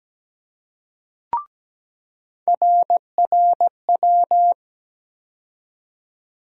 Morse code, heard as a keyed tone, about 17 wpm, 700 Hz. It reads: RRW
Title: USA Call Sign Postfixes - 3 Letters 17wpm